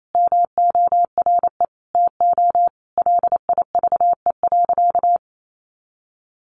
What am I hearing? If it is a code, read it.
Morse code, heard as a keyed tone, about 28 wpm, 700 Hz. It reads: MORE TO LIVE.
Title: Sentences from Top 300 Words 28wpm